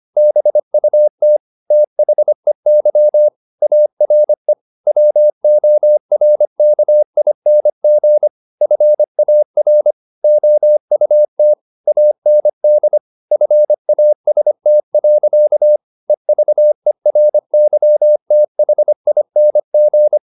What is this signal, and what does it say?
Morse code, heard as a keyed tone, about 25 wpm, 600 Hz. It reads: BUT THEY ARE WORKING FAR OUT AND FAST. EVERYTHING
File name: the old man and the sea part 1 25wpm